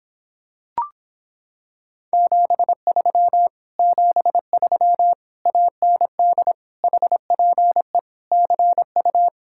Morse code, heard as a keyed tone, about 26 wpm, 700 Hz. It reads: 73 73 AND HPE CU